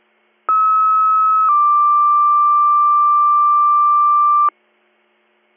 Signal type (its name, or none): two-tone page